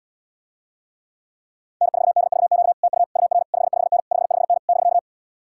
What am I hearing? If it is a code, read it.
Morse code, heard as a keyed tone, about 79 wpm, 700 Hz. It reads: N4DVQ TU UR 55N 55N <BK>